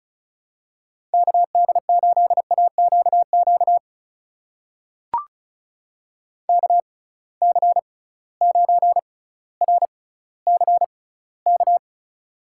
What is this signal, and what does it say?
Morse code, heard as a keyed tone, about 36 wpm, spaced as for 18 wpm, 700 Hz. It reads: KD8AQQ KC9RCK
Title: USA Call Signs - 6 Characters - 6x Character Spacing - 35wpm